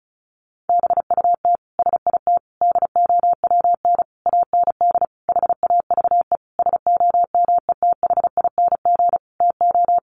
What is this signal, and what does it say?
Morse code, heard as a keyed tone, about 34 wpm, 700 Hz. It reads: BUT SIT DOWN AND HAVE SOMETHING TO
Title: Sentences from Top 600 Words 35wpm